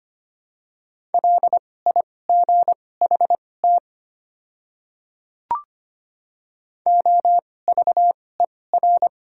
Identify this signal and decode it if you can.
Morse code, heard as a keyed tone, about 25 wpm, spaced as for 18 wpm, 690 Hz. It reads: LIGHT OVER
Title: 2x Character Spacing - Top 200 Words - 25wpm